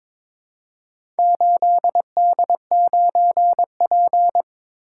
Morse code, heard as a keyed tone, about 22 wpm, 700 Hz. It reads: 8D9P
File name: practice / 22wpm / Four Letter-Number Combinations - 22wpm